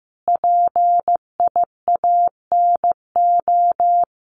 Morse code, heard as a keyed tone, about 15 wpm, 700 Hz. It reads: PIANO
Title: Additional Easy Phrases 15wpm